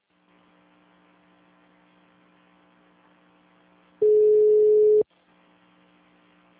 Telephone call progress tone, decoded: ringback tone

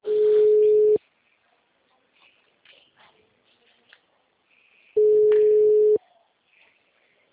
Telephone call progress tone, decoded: ringback tone

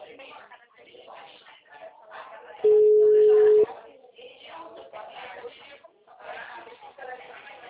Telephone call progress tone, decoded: ringback tone

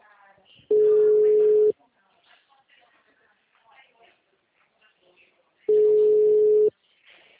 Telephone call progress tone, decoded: ringback tone

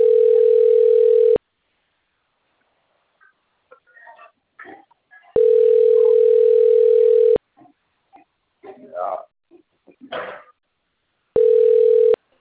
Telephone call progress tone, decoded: ringback tone

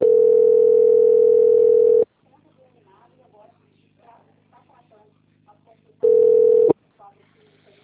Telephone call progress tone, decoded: ringback tone

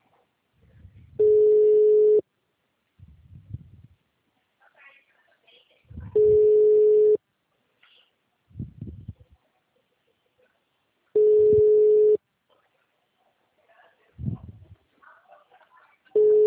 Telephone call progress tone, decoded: ringback tone

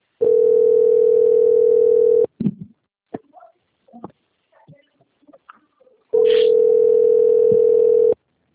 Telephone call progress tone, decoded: ringback tone

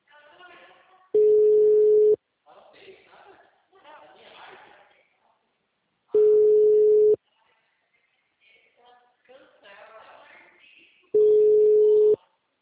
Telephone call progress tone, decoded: ringback tone

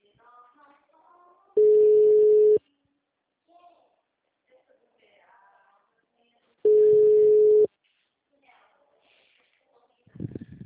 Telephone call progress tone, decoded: ringback tone

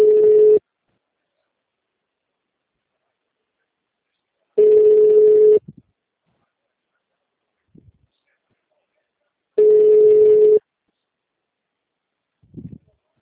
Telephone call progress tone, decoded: ringback tone

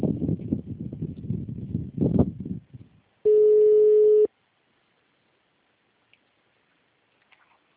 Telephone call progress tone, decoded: ringback tone

